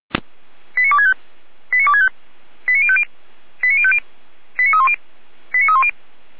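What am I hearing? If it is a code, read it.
Five-tone selective call (ZVEI), heard as quick runs of five tones, tones of about 70 ms. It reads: 89265 89265 89050 89050 89310 89310
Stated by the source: ZVEI five-tone selective call received via SDR.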